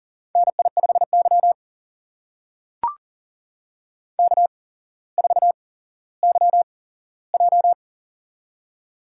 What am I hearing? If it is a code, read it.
Morse code, heard as a keyed tone, about 39 wpm, spaced as for 17 wpm, 700 Hz. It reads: NI5Y K4YJ
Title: USA Call Signs - 4 Characters - 8x Character Spacing - 40wpm